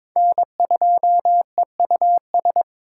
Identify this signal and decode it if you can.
Morse code, heard as a keyed tone, about 26 wpm, 700 Hz. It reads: N2EUS
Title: Speed Racing - USA 1x3 Call Signs 22wpm